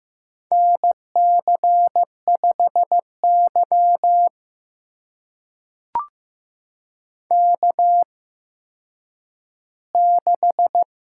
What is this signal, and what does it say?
Morse code, heard as a keyed tone, about 15 wpm, spaced as for 9 wpm, 700 Hz. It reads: NC5Y K6
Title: USA Call Signs - 4 Characters - 8x Character Spacing - 15wpm